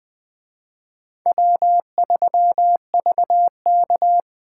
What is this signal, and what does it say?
Morse code, heard as a keyed tone, about 20 wpm, 700 Hz. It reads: W3VK
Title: Speed Racing - USA 1x2 Call Signs 17wpm